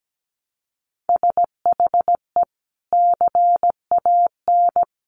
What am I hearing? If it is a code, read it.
Morse code, heard as a keyed tone, about 17 wpm, 700 Hz. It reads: SHE CAN
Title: Sentences from Top 1100 Words 17wpm